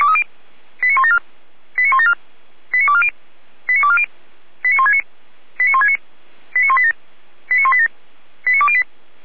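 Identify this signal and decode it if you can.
Five-tone selective call (ZVEI), heard as quick runs of five tones, tones of about 70 ms. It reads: 89174 89174 89240 89240 89169 89169 89187 89187 89208
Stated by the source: ZVEI five-tone call received on an SDR.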